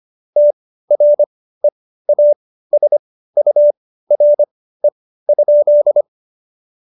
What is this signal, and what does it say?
Morse code, heard as a keyed tone, about 25 wpm, spaced as for 15 wpm, 590 Hz. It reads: TREASURE?
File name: Jokes-25-char-15-eff